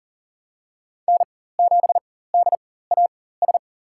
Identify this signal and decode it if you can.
Morse code, heard as a keyed tone, about 40 wpm, spaced as for 19 wpm, 690 Hz. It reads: N7DAS
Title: USA Call Signs - 4x Character Spacing - 40wpm